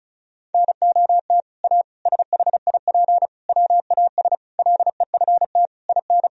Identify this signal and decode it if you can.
Morse code, heard as a keyed tone, about 36 wpm, 700 Hz. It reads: NOT A SHIP WAS LEFT IN
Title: Sentences from Top 1000 Words - Review Part 2 - 35wpm